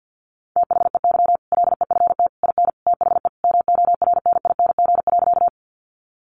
Morse code, heard as a keyed tone, about 50 wpm, 700 Hz. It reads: THEY LEFT IN THE MORNING.
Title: Sentences from Top 400 Words 50wpm